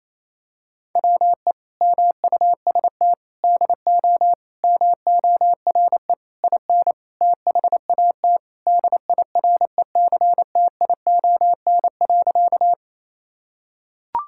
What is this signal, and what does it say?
Morse code, heard as a keyed tone, about 28 wpm, 700 Hz. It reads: WE MUST DO MORE IN THAT DIRECTION.
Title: Sentences from Top 1000 Words - Review Part 5 - 28wpm